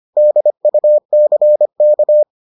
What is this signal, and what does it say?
Morse code, heard as a keyed tone, about 25 wpm, 600 Hz. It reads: DUCK